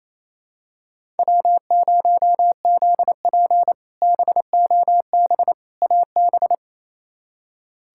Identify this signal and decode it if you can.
Morse code, heard as a keyed tone, about 28 wpm, 700 Hz. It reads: W0ZP BOB AB
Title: K1USN SST QSO Practice - 28wpm